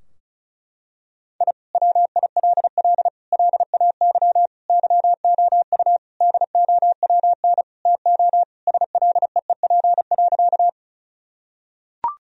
Morse code, heard as a keyed tone, about 34 wpm, 700 Hz. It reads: I WILL LAY YOU DOWN TO SLEEP.
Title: Sentences from Top 1000 Words - Review Part 2 - 35wpm